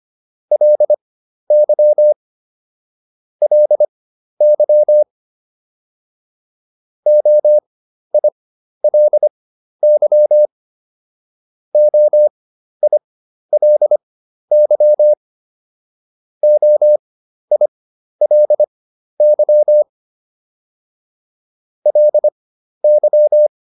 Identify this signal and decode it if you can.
Morse code, heard as a keyed tone, about 25 wpm, spaced as for 12 wpm, 600 Hz. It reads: LY LY OILY OILY OILY LY